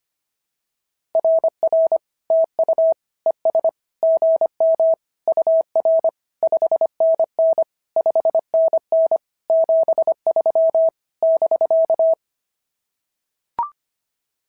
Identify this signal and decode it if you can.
Morse code, heard as a keyed tone, about 25 wpm, 650 Hz. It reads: RR TU ES GM UR 5NN 5NN 73 <BK>
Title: SOTA QSO Practice 25wpm